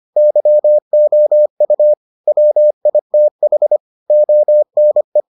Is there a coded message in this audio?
Yes: Morse code from 0.2 to 5.2 s, heard as a keyed tone, about 25 wpm, 600 Hz. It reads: YOU WITH ONE